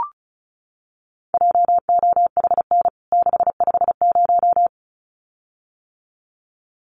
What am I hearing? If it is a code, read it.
Morse code, heard as a keyed tone, about 34 wpm, 700 Hz. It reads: JOHN 650
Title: CWT Exchange 35wpm